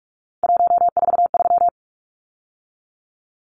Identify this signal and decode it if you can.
Morse code, heard as a keyed tone, about 44 wpm, 700 Hz. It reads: 143
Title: Numbers - 3-Digit 45wpm